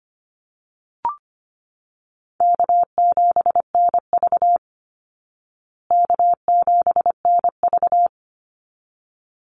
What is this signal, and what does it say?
Morse code, heard as a keyed tone, about 25 wpm, 700 Hz. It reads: K7NV K7NV
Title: Easy Call Signs Repeated 2x - 25wpm